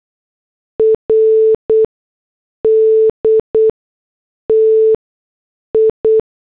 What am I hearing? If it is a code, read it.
Morse code, heard as a keyed tone, about 8 wpm, spaced as for 6 wpm, 440 Hz. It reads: RD TI
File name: morse / headlines